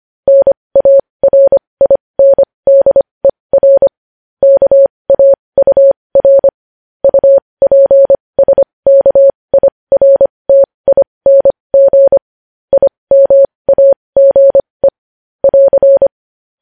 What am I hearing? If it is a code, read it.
Morse code, heard as a keyed tone, about 25 wpm, spaced as for 20 wpm, 550 Hz. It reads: NARINDER KAUR UPSKIRTING IMAGE <AR>
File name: podcast / News headlines-1x-25x20